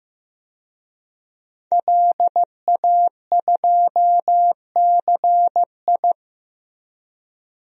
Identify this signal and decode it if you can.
Morse code, heard as a keyed tone, about 15 wpm, 700 Hz. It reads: LA2CI
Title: DXCC-100 Call Signs - 5 Characters - Repeated 2x - 15wpm